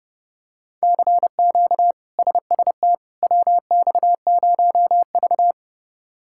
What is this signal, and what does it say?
Morse code, heard as a keyed tone, about 30 wpm, 700 Hz. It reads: CQ SST WX0V
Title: K1USN SST QSO Practice - 30wpm